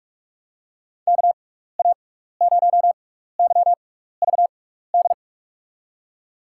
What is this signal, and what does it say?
Morse code, heard as a keyed tone, about 46 wpm, spaced as for 15 wpm, 700 Hz. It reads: KA0YVD
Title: USA Call Signs - 6 Characters - 6x Character Spacing - 45wpm